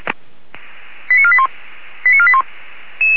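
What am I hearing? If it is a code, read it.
Five-tone selective call (ZVEI), heard as quick runs of five tones, tones of about 70 ms. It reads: 89481 89481
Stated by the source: ZVEI five-tone call received with an SDR.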